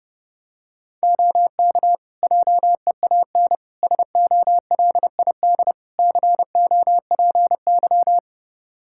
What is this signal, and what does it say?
Morse code, heard as a keyed tone, about 30 wpm, 700 Hz. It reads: OK JEAN SOLID COPY
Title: Practice QSO - Exchange 3 - More Info - 30wpm